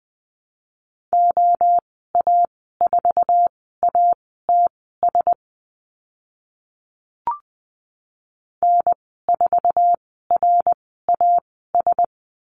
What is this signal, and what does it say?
Morse code, heard as a keyed tone, about 20 wpm, spaced as for 14 wpm, 700 Hz. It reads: OA4ATS N4RAS
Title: 062 Lesson 11g - Letter h - Call Signs 20wpm